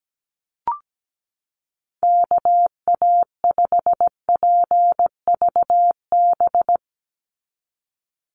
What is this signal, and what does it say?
Morse code, heard as a keyed tone, about 17 wpm, 700 Hz. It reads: KA5PVB
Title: NA SOTA Chaser Callsigns 17wpm